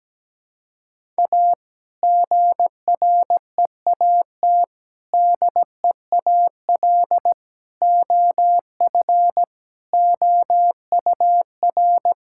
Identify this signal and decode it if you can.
Morse code, heard as a keyed tone, about 17 wpm, 700 Hz. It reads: A GREAT DEAL OF OUR
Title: Sentences from Top 1000 Words - Review Part 6 - 17wpm